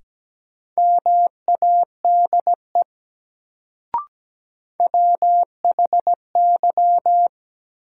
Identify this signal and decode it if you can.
Morse code, heard as a keyed tone, about 17 wpm, 700 Hz. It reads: MADE WHY
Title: Sentences from Top 200 Words - New Word Drill - 17wpm